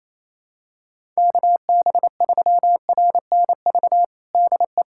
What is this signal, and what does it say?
Morse code, heard as a keyed tone, about 27 wpm, 700 Hz. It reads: KB3RNV DE
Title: USA Callsign DE Callsign 28wpm